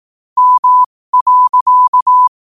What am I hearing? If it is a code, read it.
Morse code, heard as a keyed tone, about 18 wpm, spaced as for 15 wpm, 1000 Hz. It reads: M.
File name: qotdfiles / 15wpm